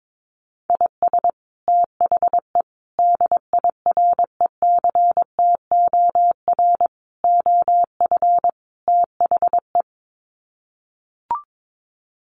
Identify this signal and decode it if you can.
Morse code, heard as a keyed tone, about 22 wpm, 700 Hz. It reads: IS THE DIRECTOR OF THE
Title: Sets of 5 Words Encore2 22wpm